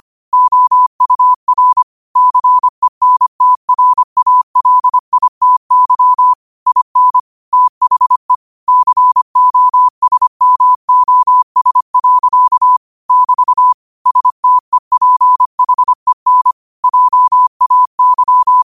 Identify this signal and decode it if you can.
Morse code, heard as a keyed tone, about 25 wpm, 1000 Hz. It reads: OUR CENTRALITY IN THE COSMOS. <BT> STEPHEN JAY
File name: qotdfiles / 25wpm